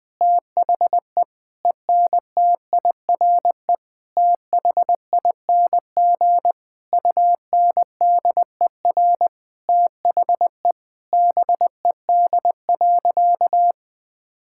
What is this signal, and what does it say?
Morse code, heard as a keyed tone, about 20 wpm, 700 Hz. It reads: THE ENTIRE THING UNDER THE BED.
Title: Sentences from Top 1000 Words - Review Part 5 - 20wpm